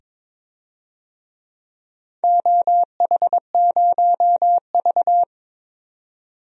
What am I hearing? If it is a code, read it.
Morse code, heard as a keyed tone, about 22 wpm, 700 Hz. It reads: OH0V
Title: DXCC-100 Call Signs - 4 Characters - 22wpm